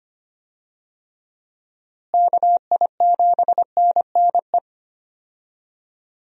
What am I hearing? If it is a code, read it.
Morse code, heard as a keyed tone, about 25 wpm, 700 Hz. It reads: KI7NNE
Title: USA Call Signs - 6 Characters - Repeated 2x - 25wpm